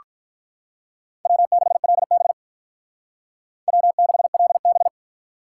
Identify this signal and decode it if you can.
Morse code, heard as a keyed tone, about 46 wpm, 700 Hz. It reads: W6LB W6LB